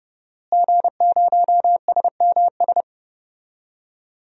Morse code, heard as a keyed tone, about 31 wpm, 700 Hz. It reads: G0SMS
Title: DXCC-100 Call Signs - 5 Characters - Repeated 2x - 30wpm